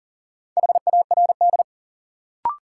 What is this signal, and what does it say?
Morse code, heard as a keyed tone, about 39 wpm, 700 Hz. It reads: HARD
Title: Sentences from Top 300 Words - New Word Drill - 40wpm